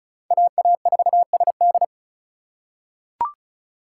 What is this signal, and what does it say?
Morse code, heard as a keyed tone, about 36 wpm, 700 Hz. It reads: AA4SD
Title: USA Call Signs - 5 Characters - 2x Character Spacing - 35wpm